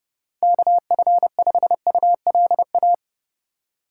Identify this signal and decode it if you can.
Morse code, heard as a keyed tone, about 32 wpm, 700 Hz. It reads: KF5ULA
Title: USA Call Signs - 6 Characters - 4x Character Spacing - 30wpm